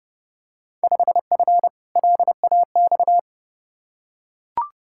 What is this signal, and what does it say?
Morse code, heard as a keyed tone, about 30 wpm, 700 Hz. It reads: 5F LAX
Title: Field Day Exchanges Encore - 30wpm